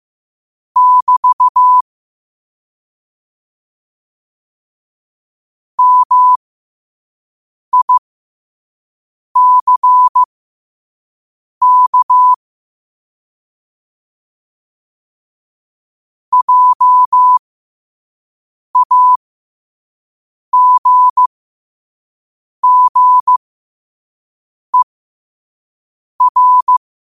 Morse code, heard as a keyed tone, about 15 wpm, spaced as for 5 wpm, 1000 Hz. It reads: <BT> MICK JAGGER
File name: 5wpm